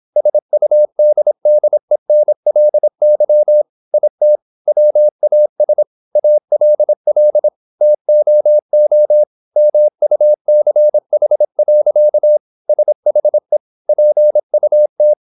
Morse code, heard as a keyed tone, about 26 wpm, 600 Hz. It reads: SUDDENLY IT WAS ALL TOO MUCH. SHE PUT
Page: Never 26wpm | CW med Gnister